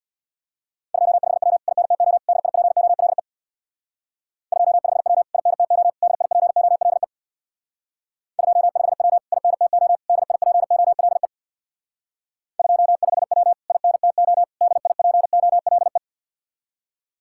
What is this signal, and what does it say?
Morse code, heard as a keyed tone, about 59 wpm, 700 Hz. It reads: 25W INTO DIPOLE 25W INTO DIPOLE 25W INTO DIPOLE 25W INTO DIPOLE
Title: Speed-Racing Common QSO Phrases Encore - 50wpm